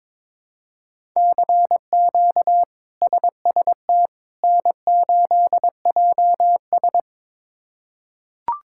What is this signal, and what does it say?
Morse code, heard as a keyed tone, about 21 wpm, 700 Hz. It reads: CQ SST N8JS
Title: K1USN SST QSO Practice - 22wpm